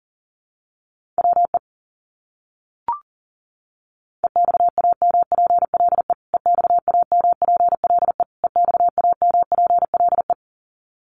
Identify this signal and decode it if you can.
Morse code, heard as a keyed tone, about 40 wpm, 700 Hz. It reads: WE EXAMPLE EXAMPLE EXAMPLE
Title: Top 300 Words Repeated 40wpm